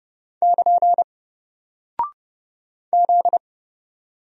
Morse code, heard as a keyed tone, about 30 wpm, 700 Hz. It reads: <KN> Z
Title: Single Character (Letter:Number:Punctuation:Prosign) 30wpm